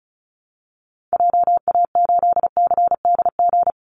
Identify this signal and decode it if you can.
Morse code, heard as a keyed tone, about 34 wpm, 700 Hz. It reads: JA8CDG
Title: Call Signs Repeated 2x - 35wpm